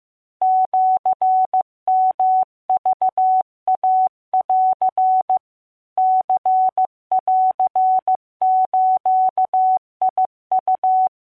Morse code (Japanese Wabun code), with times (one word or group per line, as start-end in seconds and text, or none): 0.4-5.4 シヨクイン
6.0-11.1 ニンズウ